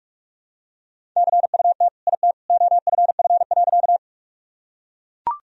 Morse code, heard as a keyed tone, about 41 wpm, 700 Hz. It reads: CUT IT OFF.
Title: Sentences from Top 1000 Words - Review Part 1 - 45wpm